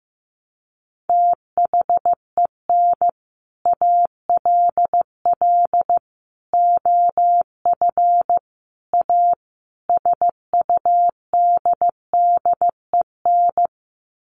Morse code, heard as a keyed tone, about 15 wpm, 700 Hz. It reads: THEN ALL OF A SUDDEN
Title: Sets of 5 Words Encore 15wpm